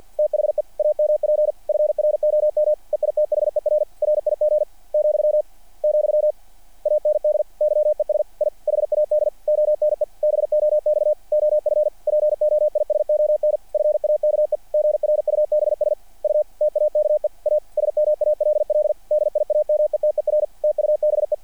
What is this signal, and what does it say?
Morse code, heard as a keyed tone, about 48 wpm, 590 Hz. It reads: THE AMP FROM EITHER RIG , , AND YES I HAD ONE BOX OF POISON FAKE GRUBS U TAKE A SMALL DIAMETER TUBE
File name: archived / CRSnet